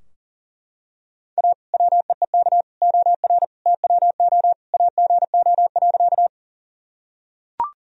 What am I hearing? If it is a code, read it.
Morse code, heard as a keyed tone, about 40 wpm, 700 Hz. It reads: A WEEK OR TWO AGO.
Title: Sentences from Top 1000 Words - Review Part 3 - 40wpm